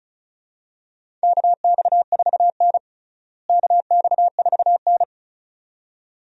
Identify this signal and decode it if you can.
Morse code, heard as a keyed tone, about 34 wpm, 700 Hz. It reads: KX4N KX4N
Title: USA Call Signs - 4 Characters - Repeated 2x - 35wpm